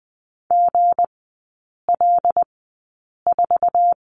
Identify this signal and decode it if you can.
Morse code, heard as a keyed tone, about 20 wpm, spaced as for 8 wpm, 690 Hz. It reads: GL4
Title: 159 Lesson 26e - Letter g - Three Characters 20wpm